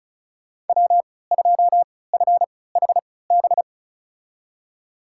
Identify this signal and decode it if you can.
Morse code, heard as a keyed tone, about 36 wpm, spaced as for 20 wpm, 700 Hz. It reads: W2FHB
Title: USA Call Signs - 3x Character Spacing - 35wpm